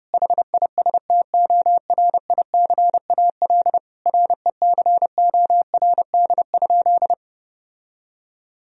Morse code, heard as a keyed tone, about 28 wpm, 700 Hz. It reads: HISTORICAL RECORD?
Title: Sentences from Top 1000 Words - Review Part 4 - 30wpm